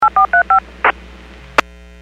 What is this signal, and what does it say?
DTMF, heard as telephone touch tones, tones of about 110 ms. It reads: 8435
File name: WPS 451.25